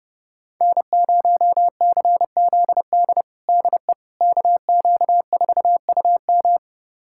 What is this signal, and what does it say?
Morse code, heard as a keyed tone, about 30 wpm, 700 Hz. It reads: N0CZD DE KQ4UM
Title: USA Callsign DE Callsign - 5 Characters - 30wpm